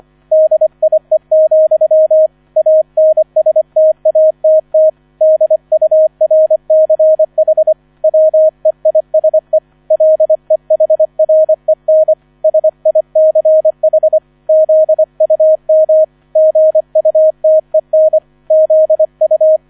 Morse code, heard as a keyed tone, about 25 wpm, 630 Hz. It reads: DIE, ANSTATT DURCH WEISE LEHREN SICH ZUM GUTEN ZU